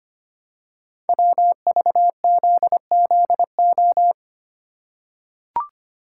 Morse code, heard as a keyed tone, about 25 wpm, 700 Hz. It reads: WVZZO
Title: Code Groups - 5 Letters - 25wpm